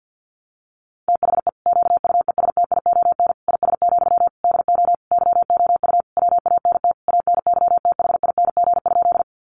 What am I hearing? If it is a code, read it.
Morse code, heard as a keyed tone, about 48 wpm, 700 Hz. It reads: THE QUESTION IS, DO YOU WANT ANYTHING?
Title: Sentences from Top 1000 Words 50wpm